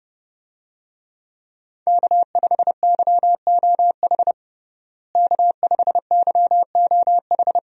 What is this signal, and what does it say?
Morse code, heard as a keyed tone, about 32 wpm, 700 Hz. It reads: K5YOH K5YOH
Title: USA Call Signs - 5 Characters - Repeated 2x - 30wpm